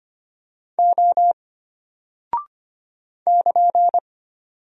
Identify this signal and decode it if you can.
Morse code, heard as a keyed tone, about 25 wpm, 700 Hz. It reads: O <KN>
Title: Single Character (Letter:Number:Punctuation:Prosign) Alt 25wpm